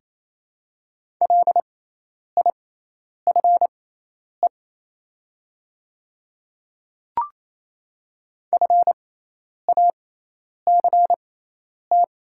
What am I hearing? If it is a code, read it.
Morse code, heard as a keyed tone, about 28 wpm, spaced as for 10 wpm, 700 Hz. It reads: LIFE FACT
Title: 6x Character Spacing - Top 500 Words - 28wpm